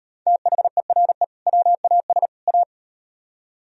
Morse code, heard as a keyed tone, about 36 wpm, 700 Hz. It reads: THERE WAS A